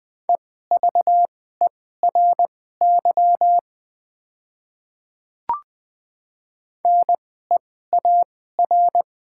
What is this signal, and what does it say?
Morse code, heard as a keyed tone, about 20 wpm, spaced as for 14 wpm, 700 Hz. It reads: EVERY NEAR
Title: 2x Character Spacing - Top 200 Words - 20wpm